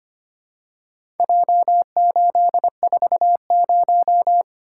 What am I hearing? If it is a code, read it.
Morse code, heard as a keyed tone, about 25 wpm, 700 Hz. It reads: J840
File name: practice / 25wpm / Four Letter-Number Combinations - 25wpm